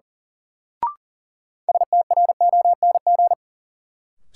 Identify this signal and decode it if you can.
Morse code, heard as a keyed tone, about 37 wpm, 700 Hz. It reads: STRONG
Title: Sentences from Top 500 Words - New Word Drill - 40wpm